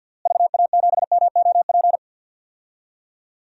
Morse code, heard as a keyed tone, about 47 wpm, 700 Hz. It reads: VA7MOP
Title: DXCC-100 Call Signs - 6 Characters - Repeated 3x - 50wpm